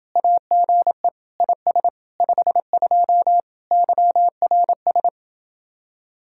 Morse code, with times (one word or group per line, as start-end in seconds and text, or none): 0.2-1.1 AGE
1.4-1.9 IS
2.2-3.4 52
3.7-5.1 YRS